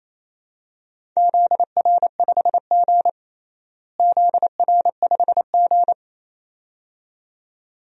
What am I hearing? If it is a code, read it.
Morse code, heard as a keyed tone, about 28 wpm, 700 Hz. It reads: ZR5G ZR5G
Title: DXCC-100 Call Signs - 4 Characters - Repeated 2x - 28wpm